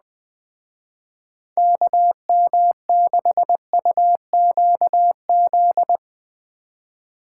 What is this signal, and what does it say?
Morse code, heard as a keyed tone, about 20 wpm, 700 Hz. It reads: KM6UQZ